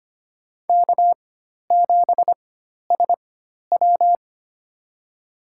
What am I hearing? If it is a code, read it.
Morse code, heard as a keyed tone, about 25 wpm, spaced as for 12 wpm, 700 Hz. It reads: K7SW
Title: USA Call Signs - 4x Character Spacing - 25wpm